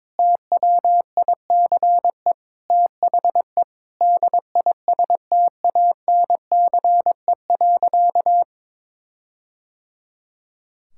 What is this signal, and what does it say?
Morse code, heard as a keyed tone, about 22 wpm, 700 Hz. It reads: TWICE THE DISTANCE.
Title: Sentences from Top 1600 Words 22wpm